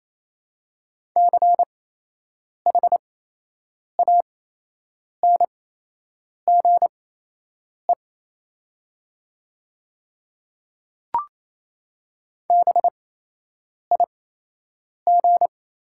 Morse code, heard as a keyed tone, about 28 wpm, spaced as for 8 wpm, 700 Hz. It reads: CHANGE BIG
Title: 8x Character Spacing - Top 200 Words - 28wpm